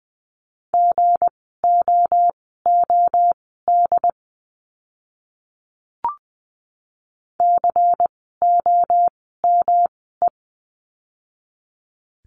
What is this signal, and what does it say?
Morse code, heard as a keyed tone, about 20 wpm, spaced as for 15 wpm, 700 Hz. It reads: GOOD COME
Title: 237 Lesson 38f - Letter q - Words 20wpm